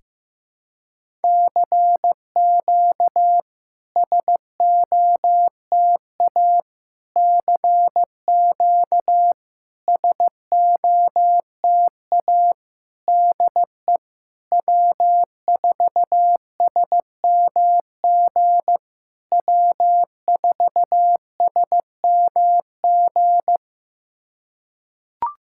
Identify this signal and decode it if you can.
Morse code, heard as a keyed tone, about 15 wpm, 700 Hz. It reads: CQ SOTA CQ SOTA DE W4SMG W4SMG
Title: SOTA QSO Practice 15wpm